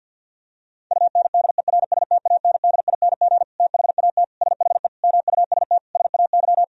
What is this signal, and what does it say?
Morse code, heard as a keyed tone, about 52 wpm, 700 Hz. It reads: UNDERSTANDING THAT SHE MUST SAY
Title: Sentences from Top 1000 Words - Review Part 3 - 50wpm